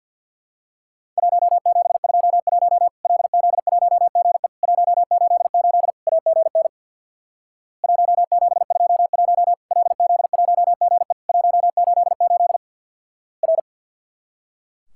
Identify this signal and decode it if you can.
Morse code, heard as a keyed tone, about 50 wpm, 700 Hz. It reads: 1721 LZ1GE 188 AGN 1721 LZ1GE 188 R
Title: Worked All Europe DX Contest - QTC Exchange with Repeats - 50wpm